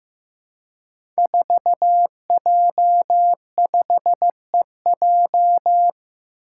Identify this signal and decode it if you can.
Morse code, heard as a keyed tone, about 15 wpm, 700 Hz. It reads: 4J5EJ
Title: Five Challenging Letter-Number Combinations 15wpm